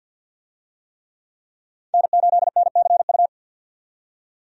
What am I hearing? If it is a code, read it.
Morse code, heard as a keyed tone, about 47 wpm, 700 Hz. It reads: N8NCU
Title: USA Call Signs - 5 Characters - 50wpm